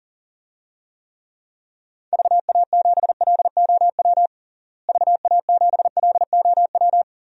Morse code, heard as a keyed tone, about 40 wpm, 700 Hz. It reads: VA7LOW VA7LOW
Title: DXCC-100 Call Signs - 6 Characters - Repeated 3x - 40wpm